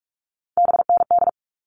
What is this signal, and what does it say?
Morse code, heard as a keyed tone, about 44 wpm, 700 Hz. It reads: BND
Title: All Cut Numbers - 3-Digit 45wpm